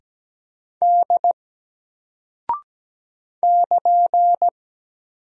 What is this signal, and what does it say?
Morse code, heard as a keyed tone, about 17 wpm, 700 Hz. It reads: D <KN>
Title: Single Character (Letter:Number:Punctuation:Prosign) 17wpm